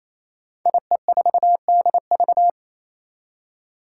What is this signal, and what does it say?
Morse code, heard as a keyed tone, about 27 wpm, 700 Hz. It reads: IE4DV
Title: Five Challenging Letter-Number Combinations 28wpm